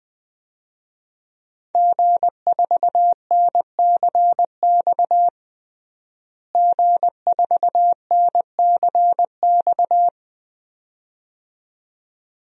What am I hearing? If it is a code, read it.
Morse code, heard as a keyed tone, about 20 wpm, 700 Hz. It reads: G4NCX G4NCX
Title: DXCC-100 Call Signs - 5 Characters - Repeated 2x - 20wpm